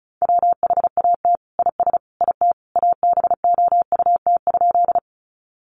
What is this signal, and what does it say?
Morse code, heard as a keyed tone, about 34 wpm, 700 Hz. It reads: WHAT IS IT ABOUT?
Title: Sentences from Top 100 Words 35wpm